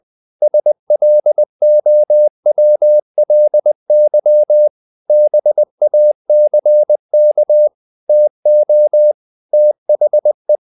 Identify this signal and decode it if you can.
Morse code, heard as a keyed tone, about 20 wpm, 600 Hz. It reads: SLOWLY BACK TO THE
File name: the old man and the sea part 2 20wpm